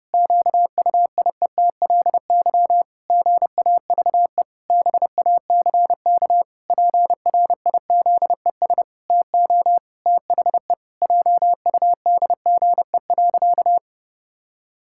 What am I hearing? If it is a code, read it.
Morse code, heard as a keyed tone, about 30 wpm, 700 Hz. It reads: QUIETLY GAVE BACK PRIZES TO THE JUDGE.
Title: Pangram Challenge v2 - 30wpm